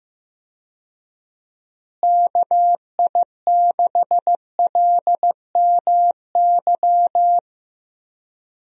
Morse code, heard as a keyed tone, about 15 wpm, 700 Hz. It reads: KI6LMY
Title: USA Call Signs - 6 Characters - Repeated 3x - 15wpm